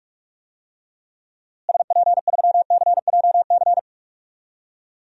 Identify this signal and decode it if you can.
Morse code, heard as a keyed tone, about 44 wpm, 700 Hz. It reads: SP3CJC
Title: DXCC-100 Call Signs - 6 Characters - Repeated 3x - 45wpm